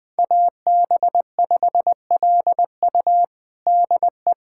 Morse code, heard as a keyed tone, about 21 wpm, 700 Hz. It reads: AB5LU DE
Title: USA Callsign DE Callsign 20wpm